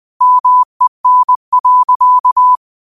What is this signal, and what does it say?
Morse code, heard as a keyed tone, about 20 wpm, 1000 Hz. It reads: MEN.
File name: qotdfiles / 20wpm